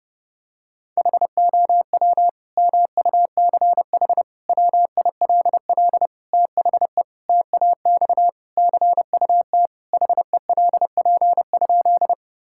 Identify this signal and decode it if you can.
Morse code, heard as a keyed tone, about 30 wpm, 700 Hz. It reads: HOW MUCH WILL THE TAX CUT HELP?
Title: Sentences from Top 1000 Words - Review Part 5 - 30wpm